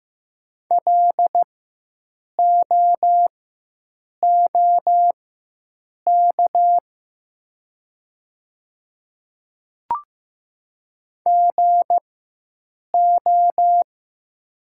Morse code, heard as a keyed tone, about 15 wpm, spaced as for 7 wpm, 700 Hz. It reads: LOOK GO
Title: 4x Character Spacing - Top 200 Words - 15wpm